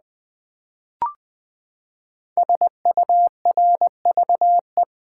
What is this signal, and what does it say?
Morse code, heard as a keyed tone, about 20 wpm, 700 Hz. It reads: SURVE
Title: Build A Word - 5 Letters At A Time - 12-Letter Words - 20wpm